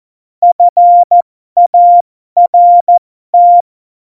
Morse code, heard as a keyed tone, about 14 wpm, spaced as for 12 wpm, 700 Hz. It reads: FART
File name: Morsekurs / Witze